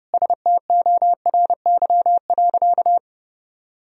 Morse code, heard as a keyed tone, about 30 wpm, 700 Hz. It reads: STORY.
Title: Sentences from Top 300 Words and Abbreviations 30wpm